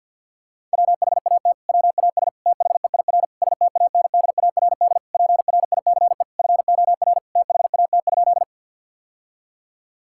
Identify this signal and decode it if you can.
Morse code, heard as a keyed tone, about 48 wpm, 700 Hz. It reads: WHAT WAS THEIR STANDARD PRICE FOR THAT?
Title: Sentences from Top 1000 Words - Review Part 6 - 50wpm